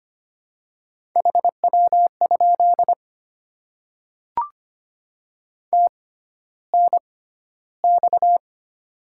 Morse code, heard as a keyed tone, about 25 wpm, spaced as for 13 wpm, 700 Hz. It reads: HW? TNX
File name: Top 20 QSO Elements - 6x Character Spacing - 25wpm